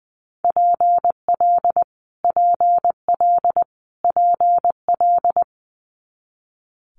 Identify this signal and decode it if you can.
Morse code, heard as a keyed tone, about 20 wpm, 700 Hz. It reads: PL PL PL
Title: Multi-letter Phonemes Repeated 3x 20wpm